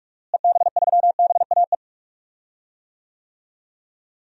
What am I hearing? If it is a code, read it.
Morse code, heard as a keyed tone, about 46 wpm, 700 Hz. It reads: EB3BAE
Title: DXCC-100 Call Signs - 6 Characters - Repeated 3x - 45wpm